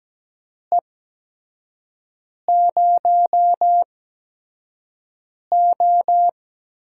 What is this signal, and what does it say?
Morse code, heard as a keyed tone, about 17 wpm, spaced as for 5 wpm, 700 Hz. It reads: E0O
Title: USA Call Signs - 4 Characters - 8x Character Spacing - 17wpm